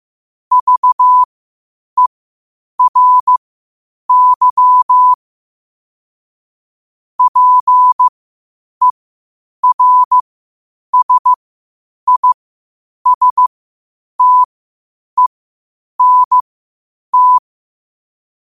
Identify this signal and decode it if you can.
Morse code, heard as a keyed tone, about 15 wpm, spaced as for 8 wpm, 1000 Hz. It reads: VERY PERSISTENT
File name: qotdfiles / 8wpm